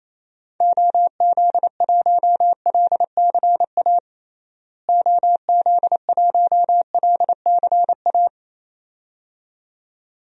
Morse code, heard as a keyed tone, about 28 wpm, 700 Hz. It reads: OZ1LCA OZ1LCA
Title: DXCC-100 Call Signs - 6 Characters - Repeated 2x - 28wpm